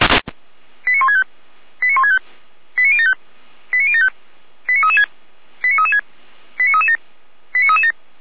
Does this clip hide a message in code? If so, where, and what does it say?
Yes: five-tone selective call (ZVEI) from 0.9 to 7.9 s, heard as quick runs of five tones, tones of about 70 ms. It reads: 89165 89165 89075 89075 89336 89336 89337 89337